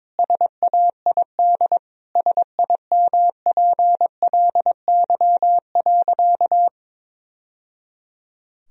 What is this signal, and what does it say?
Morse code, heard as a keyed tone, about 22 wpm, 700 Hz. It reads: SAID SIMPLY.
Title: Sentences from Top 1000 Words - Review Part 4 - 22wpm